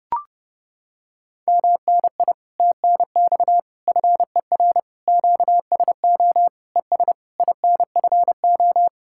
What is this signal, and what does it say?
Morse code, heard as a keyed tone, about 31 wpm, 700 Hz. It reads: MNI TNX FER QSO ES INFO